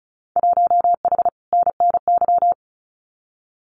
Morse code, heard as a keyed tone, about 34 wpm, 700 Hz. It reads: 1H NNY
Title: Winter Field Day Exchanges 35wpm